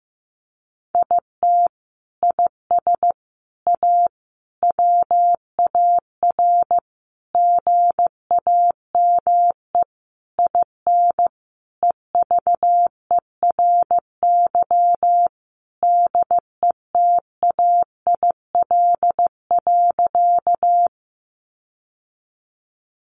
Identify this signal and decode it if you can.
Morse code, heard as a keyed tone, about 15 wpm, 700 Hz. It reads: IT IS A WAR GAME IN EVERY DETAIL.
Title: Sentences from Top 1500 Words 15wpm